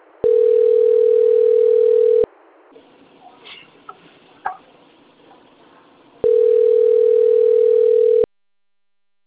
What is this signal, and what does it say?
Ringback tone (North America), 440 + 480 Hz.